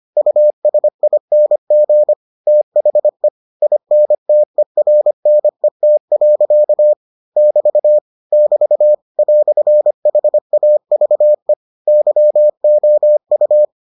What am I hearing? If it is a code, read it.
Morse code, heard as a keyed tone, about 25 wpm, 600 Hz. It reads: USING THE INTERNET. <BT> <BT> "HAVE YOU